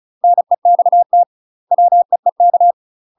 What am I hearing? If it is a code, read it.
Morse code, heard as a keyed tone, about 36 wpm, 700 Hz. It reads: NEXT WEEK
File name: UK MASTER QSOs TRAINING / 35WPM / UK MASTER QSOs 0036